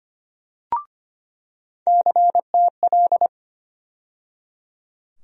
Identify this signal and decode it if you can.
Morse code, heard as a keyed tone, about 25 wpm, 700 Hz. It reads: CTL
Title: USA Call Sign Postfixes - 3 Letters 25wpm